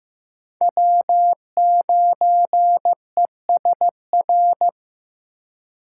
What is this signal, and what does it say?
Morse code, heard as a keyed tone, about 15 wpm, 700 Hz. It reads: W9ESR
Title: USA Call Signs - 3x Character Spacing - 15wpm